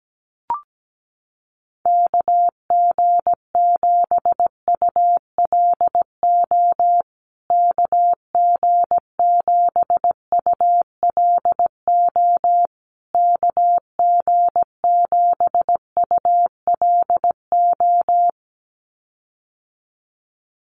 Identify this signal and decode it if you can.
Morse code, heard as a keyed tone, about 17 wpm, 700 Hz. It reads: KG7ULO KG7ULO KG7ULO
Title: USA Call Signs Repeated 3x 17wpm